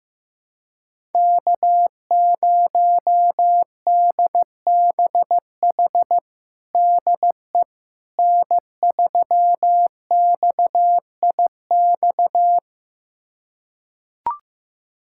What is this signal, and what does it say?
Morse code, heard as a keyed tone, about 15 wpm, 700 Hz. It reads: K0DBH DE N3XIX
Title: USA Callsign DE Callsign - 5 Characters - 15wpm